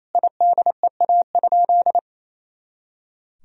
Morse code, heard as a keyed tone, about 27 wpm, 700 Hz. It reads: IDEA?
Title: Sentences from Top 400 Words and Abbreviations 28wpm